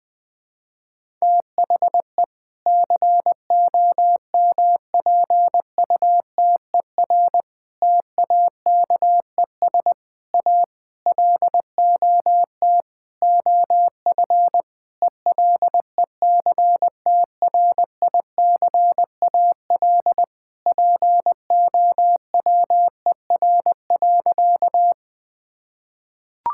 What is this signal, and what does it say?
Morse code, heard as a keyed tone, about 20 wpm, 700 Hz. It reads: THE COMPUTER TAKES A LOT OF ELECTRICAL POWER.
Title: Sentences from Top 1000 Words - Review Part 6 - 20wpm